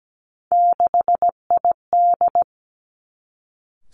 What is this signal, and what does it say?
Morse code, heard as a keyed tone, about 17 wpm, 700 Hz. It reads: 6ID
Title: Three Challenging Letter-Number Combinations 17wpm